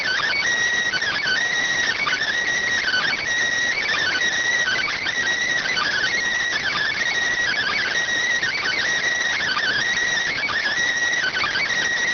Emergency Alert System (SAME) tones (header bursts), heard as harsh buzzing bursts. It is heard from 0.9 to 11.7 s.